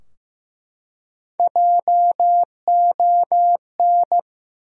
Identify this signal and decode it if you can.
Morse code, heard as a keyed tone, about 15 wpm, 700 Hz. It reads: JON
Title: USA Common Names - Max 4 Letters - 15wpm